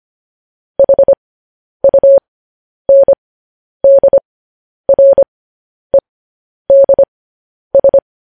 Morse code, heard as a keyed tone, about 25 wpm, spaced as for 10 wpm, 550 Hz. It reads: HUNDREDS
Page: news-headlines-in-morse-code-at-10-wpm